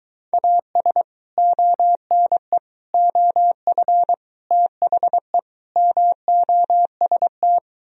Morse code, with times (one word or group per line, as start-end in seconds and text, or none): 0.3-1.0 AS
1.4-2.6 ONE
2.9-4.1 OF
4.5-5.4 THE
5.8-7.6 MOST